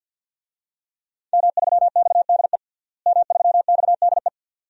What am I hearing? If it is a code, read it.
Morse code, heard as a keyed tone, about 50 wpm, 700 Hz. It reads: M3XDE M3XDE
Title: DXCC-100 Call Signs - 5 Characters - Repeated 2x - 50wpm